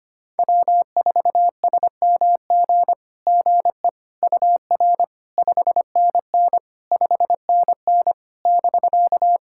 Morse code, heard as a keyed tone, about 25 wpm, 700 Hz. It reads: W4SMG GE UR 5NN 5NN <BK>
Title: SOTA QSO Practice 25wpm